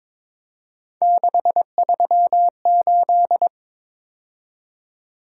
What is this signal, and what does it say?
Morse code, heard as a keyed tone, about 26 wpm, 700 Hz. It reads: 638
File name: Speed-Racing - Three Letter-Number Combinations - 22wpm